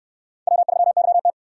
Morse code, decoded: W3YT

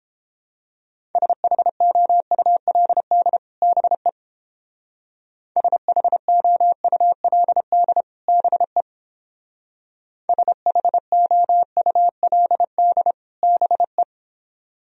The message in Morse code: SHOULD BE SHOULD BE SHOULD BE

